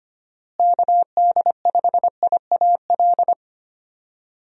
Morse code, heard as a keyed tone, about 25 wpm, 700 Hz. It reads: KD5IAL
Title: USA Call Signs - 6 Characters - Repeated 2x - 25wpm